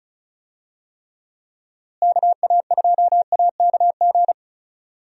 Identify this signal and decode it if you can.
Morse code, heard as a keyed tone, about 34 wpm, 700 Hz. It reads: KA2AKG
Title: USA Call Signs - 6 Characters - Repeated 3x - 35wpm